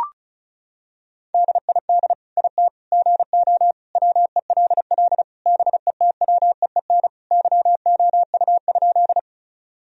Morse code, heard as a keyed tone, about 34 wpm, 700 Hz. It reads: DID IT GO WELL BETWEEN YOU?
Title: Sentences from Top 1000 Words - Review Part 1 - 35wpm